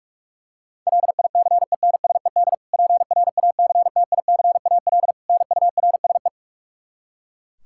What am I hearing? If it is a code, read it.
Morse code, heard as a keyed tone, about 46 wpm, 700 Hz. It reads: LICENSED PRACTICAL NURSE
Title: Practice QSO - Exchange 3 - More Info - 45wpm